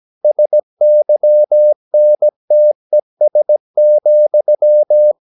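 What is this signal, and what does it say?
Morse code, heard as a keyed tone, about 17 wpm, 600 Hz. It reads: SYNTES,
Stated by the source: Slow (QRS) Morse practice at 17 wpm, from CW med Gnister.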